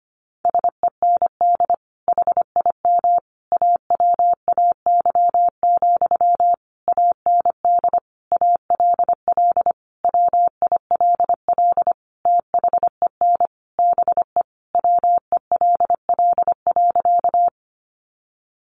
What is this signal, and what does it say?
Morse code, heard as a keyed tone, about 25 wpm, 700 Hz. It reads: SEND HIM AWAY, AND ALL WILL THEN BE WELL.
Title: Sentences from Top 1500 Words 25wpm